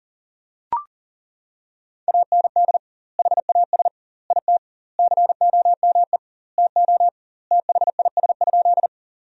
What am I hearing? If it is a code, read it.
Morse code, heard as a keyed tone, about 40 wpm, 700 Hz. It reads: AND HAS IT COME TO THIS?
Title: 2x Word-Spacing - Sentences from Top 100 Words - 40wpm